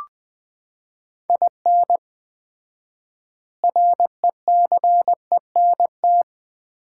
Morse code, heard as a keyed tone, about 20 wpm, 700 Hz. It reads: IN RECENT
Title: 4x Word-Spacing - Sets of 3 Words - 20wpm